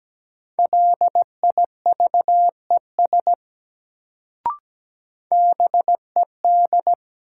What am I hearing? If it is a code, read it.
Morse code, heard as a keyed tone, about 17 wpm, 700 Hz. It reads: LIVES BED
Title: Sets of 5 Words Encore - New Word Drill - 17wpm